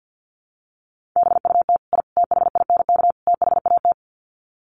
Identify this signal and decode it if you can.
Morse code, heard as a keyed tone, about 48 wpm, 700 Hz. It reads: BUT I THINK THAT
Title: Sets of 4 Words 50wpm